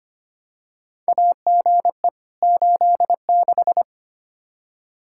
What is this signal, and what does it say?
Morse code, heard as a keyed tone, about 25 wpm, 700 Hz. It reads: AGE 86
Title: K1USN SST QSO Practice 25wpm